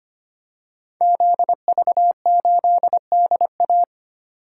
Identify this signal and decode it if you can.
Morse code, heard as a keyed tone, about 25 wpm, 700 Hz. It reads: ZV8DA
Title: Code Groups - 5 Letter:Numbers - 25wpm